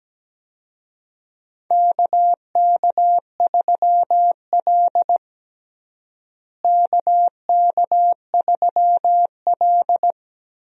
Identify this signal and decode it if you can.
Morse code, heard as a keyed tone, about 17 wpm, 700 Hz. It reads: KK3L KK3L
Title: USA Call Signs - 4 Characters - Repeated 2x - 17wpm